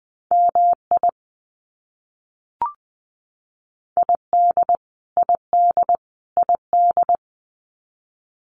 Morse code, heard as a keyed tone, about 20 wpm, 700 Hz. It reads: MI ID ID ID
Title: US States Repeated 3x 20wpm